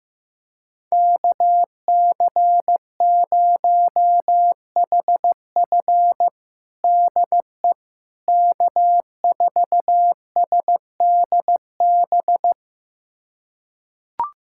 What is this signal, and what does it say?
Morse code, heard as a keyed tone, about 15 wpm, 700 Hz. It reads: KC0HF DE K4SDB
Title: USA Callsign DE Callsign - 5 Characters - 15wpm